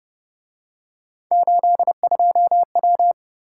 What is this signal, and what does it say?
Morse code, heard as a keyed tone, about 29 wpm, 700 Hz. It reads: 82W
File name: Speed-Racing - Three Letter-Number Combinations - 25wpm